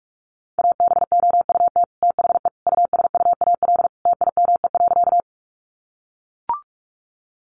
Morse code, heard as a keyed tone, about 44 wpm, 700 Hz. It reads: ABOUT THE USUAL TIME.
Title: Sentences from Top 400 Words 45wpm